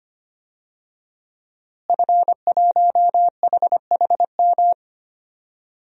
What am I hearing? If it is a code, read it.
Morse code, heard as a keyed tone, about 25 wpm, 700 Hz. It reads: F1HHM
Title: DXCC-100 Call Signs - 5 Characters - Repeated 2x - 25wpm